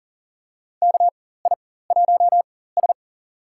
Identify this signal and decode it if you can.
Morse code, heard as a keyed tone, about 38 wpm, spaced as for 18 wpm, 700 Hz. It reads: KI1S
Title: USA Call Signs - 4 Characters - 4x Character Spacing - 40wpm